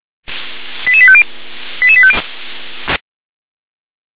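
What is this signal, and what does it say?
Five-tone selective call (ZVEI), heard as quick runs of five tones, tones of about 70 ms. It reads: 88850 88850
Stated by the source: ZVEI five-tone selective call received by SDR.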